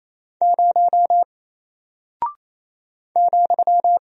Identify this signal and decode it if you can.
Morse code, heard as a keyed tone, about 28 wpm, 700 Hz. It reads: 0 ,
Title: Single Character (Letter:Number:Punctuation:Prosign) 28wpm